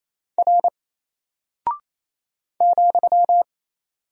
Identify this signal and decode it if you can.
Morse code, heard as a keyed tone, about 27 wpm, 700 Hz. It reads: R ,
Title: Single Character (Letter:Number:Punctuation:Prosign) 28wpm